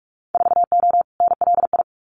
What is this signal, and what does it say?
Morse code, heard as a keyed tone, about 44 wpm, 690 Hz. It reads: 4O NLI
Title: Winter Field Day Exchanges Encore 45wpm